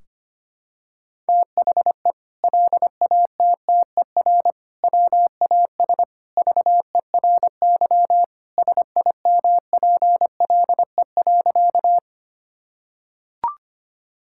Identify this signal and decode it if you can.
Morse code, heard as a keyed tone, about 25 wpm, 700 Hz. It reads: THE LATTER WAS VERY SIMPLE.
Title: Sentences from Top 1600 Words 25wpm